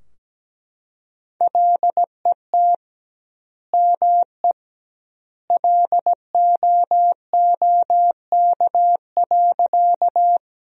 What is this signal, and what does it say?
Morse code, heard as a keyed tone, about 17 wpm, 700 Hz. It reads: LET ME LOOK.
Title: 2x Word-Spacing - Sentences from Top 300 Words - 17wpm